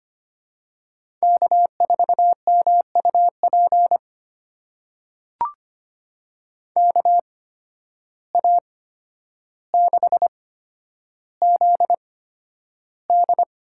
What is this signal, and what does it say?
Morse code, heard as a keyed tone, about 25 wpm, spaced as for 11 wpm, 700 Hz. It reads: K4MUP KA6ZD
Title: USA Call Signs - 8x Character Spacing - 25wpm